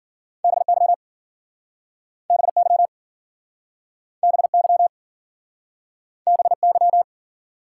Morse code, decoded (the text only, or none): BY BY BY BY